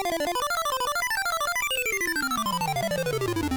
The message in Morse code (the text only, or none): none